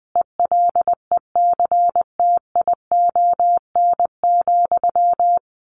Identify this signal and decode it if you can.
Morse code, heard as a keyed tone, about 20 wpm, 700 Hz. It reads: ELECTION,
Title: Sentences from Top 1500 Words 20wpm